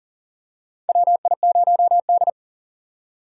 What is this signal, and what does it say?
Morse code, heard as a keyed tone, about 40 wpm, 700 Hz. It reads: WI0D
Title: USA Call Signs - 4 Characters - 40wpm